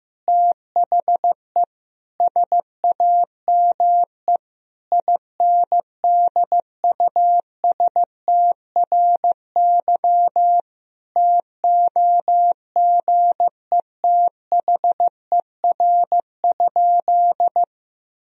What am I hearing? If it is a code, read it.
Morse code, heard as a keyed tone, about 15 wpm, 700 Hz. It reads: THE SAME INDUSTRY TOGETHER?
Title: Sentences from Top 1000 Words - Review Part 4 - 15wpm